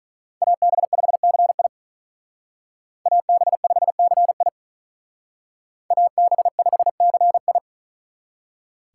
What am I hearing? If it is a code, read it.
Morse code, heard as a keyed tone, about 43 wpm, 700 Hz. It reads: AB5CI AB5CI AB5CI
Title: Speed Racing - USA 2x2 Call Signs 35wpm